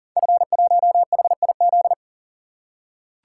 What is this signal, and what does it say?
Morse code, heard as a keyed tone, about 43 wpm, 700 Hz. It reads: F1HIZ
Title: Code Groups - 5 Letter:Numbers - 40wpm